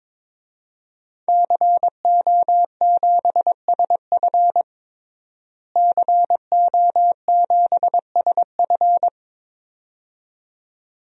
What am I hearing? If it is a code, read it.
Morse code, heard as a keyed tone, about 22 wpm, 700 Hz. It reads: CO7SF CO7SF
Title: DXCC-100 Call Signs - 5 Characters - Repeated 2x - 22wpm